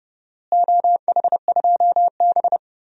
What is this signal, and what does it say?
Morse code, heard as a keyed tone, about 30 wpm, 700 Hz. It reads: OH2B
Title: DXCC-100 Call Signs - 4 Characters - Repeated 2x - 30wpm